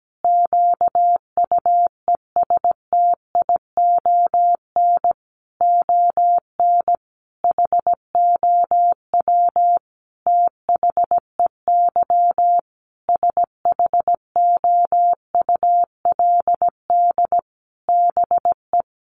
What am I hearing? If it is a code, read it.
Morse code, heard as a keyed tone, about 17 wpm, 700 Hz. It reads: QUESTION ON HOW THEY SHOULD BE
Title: Sentences from Top 1300 Words 17wpm